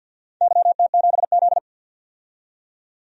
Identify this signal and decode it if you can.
Morse code, heard as a keyed tone, about 49 wpm, 700 Hz. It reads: YT7Z
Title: DXCC-100 Call Signs - 4 Characters - 50wpm